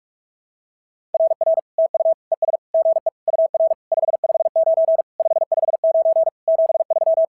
Morse code, heard as a keyed tone, about 46 wpm, 650 Hz. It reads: RR TU ES GE UR 559 559 73
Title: SOTA QSO Practice 45wpm